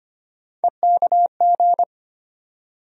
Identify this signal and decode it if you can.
Morse code, heard as a keyed tone, about 25 wpm, 700 Hz. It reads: EKG
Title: USA Call Sign Postfixes - 3 Letters 25wpm